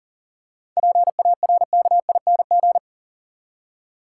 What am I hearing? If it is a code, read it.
Morse code, heard as a keyed tone, about 40 wpm, 700 Hz. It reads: PARKING
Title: Seven Letter Words - 40wpm